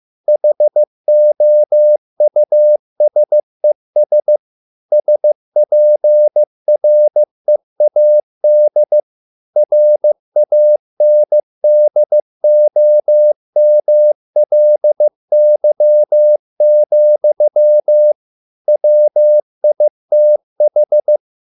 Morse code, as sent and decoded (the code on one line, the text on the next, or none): .... --- ..- ... ... / ... .--. .-. .- -.. / .-. .- -. -.. --- -- .-.. -.-- --..-- / .-- .. - ....
HOUSES SPREAD RANDOMLY, WITH